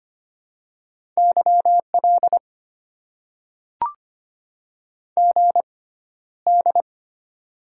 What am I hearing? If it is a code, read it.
Morse code, heard as a keyed tone, about 25 wpm, spaced as for 13 wpm, 700 Hz. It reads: YL GD
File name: Top 90 QSO Elements - 6x Character Spacing - 25wpm